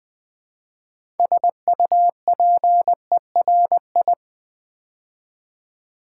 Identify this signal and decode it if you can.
Morse code, heard as a keyed tone, about 20 wpm, 700 Hz. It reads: SUPERI
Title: Build A Word - 2 Letters At A Time - 8-Letter Words - 20wpm